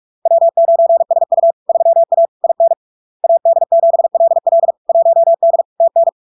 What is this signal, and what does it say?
Morse code, heard as a keyed tone, about 46 wpm, 670 Hz. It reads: W9SU 3A IN AD7LL 1D TN
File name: LARK-FD-2022-Log-in-CW-45wpm-670-Hz